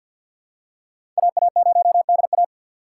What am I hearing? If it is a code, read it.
Morse code, heard as a keyed tone, about 53 wpm, 700 Hz. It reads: AA0DA
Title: USA Call Signs - 4x Character Spacing - 50wpm